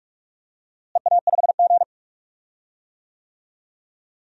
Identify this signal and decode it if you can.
Morse code, heard as a keyed tone, about 42 wpm, 700 Hz. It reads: EA5G
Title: DXCC-100 Call Signs - 4 Characters - Repeated 2x - 45wpm